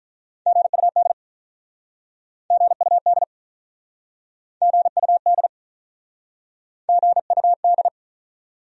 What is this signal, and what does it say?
Morse code, heard as a keyed tone, about 47 wpm, 700 Hz. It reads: GUD GUD GUD GUD